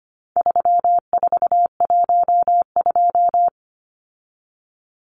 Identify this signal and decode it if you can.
Morse code, heard as a keyed tone, about 25 wpm, 700 Hz. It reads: 3412
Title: Numbers - 4-Digit 25wpm